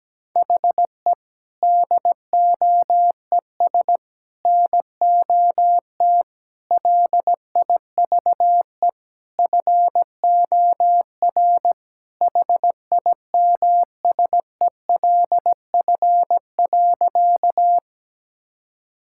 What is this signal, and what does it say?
Morse code, heard as a keyed tone, about 17 wpm, 700 Hz. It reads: HE DOES NOT LIVE FOR HIMSELF.
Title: Sentences from Top 1600 Words 17wpm